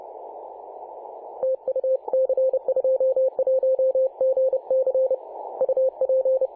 Morse code, heard as a keyed tone, about 30 wpm, 520 Hz. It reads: TU C21GC UP
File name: c21gc-24-895mhz-cw